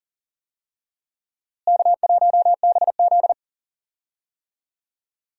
Morse code, heard as a keyed tone, about 37 wpm, 700 Hz. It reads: K1BZ
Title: USA Call Signs - 4 Characters - Repeated 2x - 40wpm